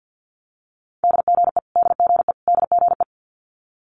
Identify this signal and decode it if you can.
Morse code, heard as a keyed tone, about 50 wpm, 700 Hz. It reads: DGE DGE DGE
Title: Multi-letter Phonemes Repeated 3x 50wpm